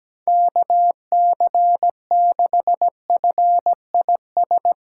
Morse code, heard as a keyed tone, about 17 wpm, 700 Hz. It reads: KC6FIS
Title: USA Callsign DE Callsign - 6 Characters - 17wpm